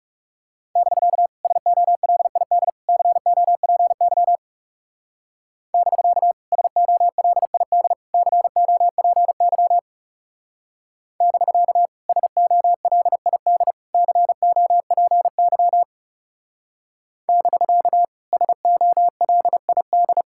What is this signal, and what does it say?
Morse code, heard as a keyed tone, about 39 wpm, 700 Hz. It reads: <BK> SOLID COPY <BK> SOLID COPY <BK> SOLID COPY <BK> SOLID